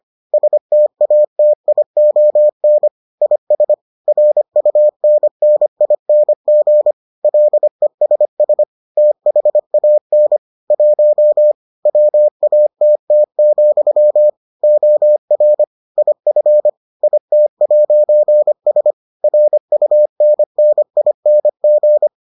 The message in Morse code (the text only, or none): STATION IS RUNNING LESS THAN 1 WATT, OR IF IT'S RUNNING